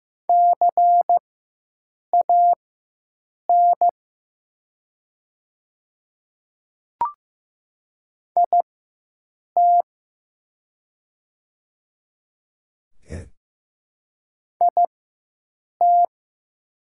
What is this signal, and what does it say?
Morse code, heard as a keyed tone, about 15 wpm, spaced as for 7 wpm, 700 Hz. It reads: CAN IT IT
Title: 4x Character Spacing - Top 100 Words - 15wpm